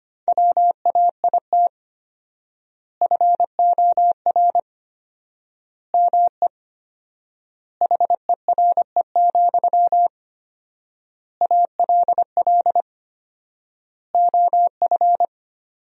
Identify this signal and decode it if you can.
Morse code, heard as a keyed tone, about 25 wpm, 700 Hz. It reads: WAIT FOR ME HERE, ALL OF
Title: 4x Word-Spacing - Sentences from Top 500 Words - 25wpm